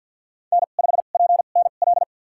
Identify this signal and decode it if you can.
Morse code, heard as a keyed tone, about 47 wpm, spaced as for 35 wpm, 700 Hz. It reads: N5PNL
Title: USA Call Signs - 2x Character Spacing - 50wpm